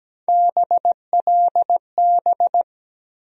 Morse code, heard as a keyed tone, about 17 wpm, 700 Hz. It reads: BLB
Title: USA Call Sign Postfixes - 3 Letters 17wpm